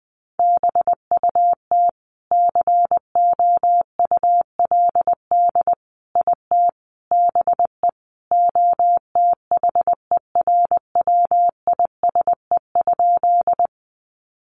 Morse code, heard as a keyed tone, about 20 wpm, 700 Hz. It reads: BUT COULD IT BE OTHERWISE?
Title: Sentences from Top 1200 Words 20wpm